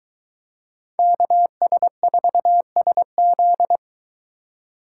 Morse code, heard as a keyed tone, about 23 wpm, 700 Hz. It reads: KS4SZ